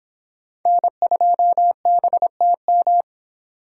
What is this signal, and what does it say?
Morse code, heard as a keyed tone, about 26 wpm, 700 Hz. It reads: N2BTM